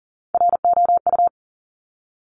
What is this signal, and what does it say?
Morse code, heard as a keyed tone, about 40 wpm, 700 Hz. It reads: ROU
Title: Top Three Letter Combinations in Words 40wpm